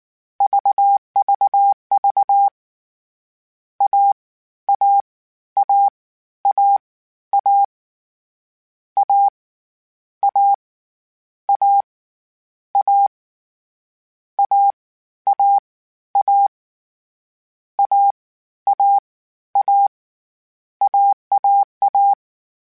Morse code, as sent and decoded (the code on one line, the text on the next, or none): ...- ...- ...- / .- / .- / .- / .- / .- / .- / .- / .- / .- / .- / .- / .- / .- / .- / .- / .- .- .-
VVV A A A A A A A A A A A A A A A AAA